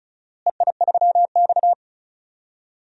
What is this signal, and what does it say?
Morse code, heard as a keyed tone, about 35 wpm, 700 Hz. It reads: EI3X
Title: DXCC-100 Call Signs - 4 Characters - Repeated 3x - 35wpm